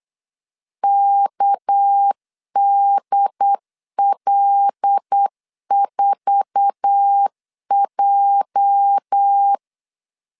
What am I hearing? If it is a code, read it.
Morse code, heard as a keyed tone, about 8 wpm, 800 Hz. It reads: KDL4J
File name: CW / DET / 1 9 1500C S B 8wpm 5624